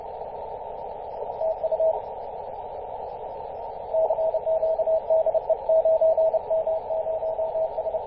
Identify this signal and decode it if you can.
Morse code, heard as a keyed tone, about 30 wpm, 640 Hz, with radio noise. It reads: TU CQ DE 9M0S
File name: 9m0s-21-026mhz-cw